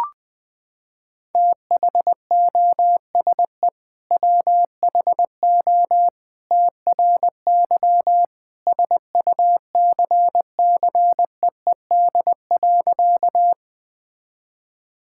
Morse code, heard as a keyed tone, about 20 wpm, 700 Hz. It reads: THOSE WHO TRY SUCCEED.
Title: Sentences with 2-Word Phrases - 20wpm